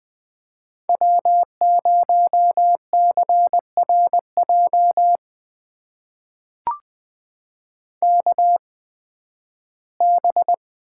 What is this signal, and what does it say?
Morse code, heard as a keyed tone, about 20 wpm, spaced as for 13 wpm, 700 Hz. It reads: W0CRJ KB
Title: USA Call Signs - 8x Character Spacing - 20wpm